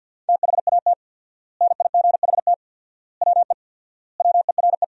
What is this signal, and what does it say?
Morse code, heard as a keyed tone, about 50 wpm, 700 Hz. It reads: THAT NIGHT WE WERE
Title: 4x Word-Spacing - Sentences from Top 300 Words - 50wpm